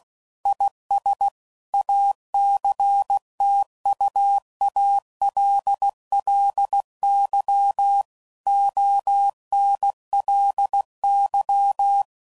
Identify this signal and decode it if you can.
Morse code, heard as a keyed tone, about 16 wpm, 800 Hz. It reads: IS ACTUALLY ONLY
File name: CW Code Groups @15WPM / Technical Subjects / AND Gate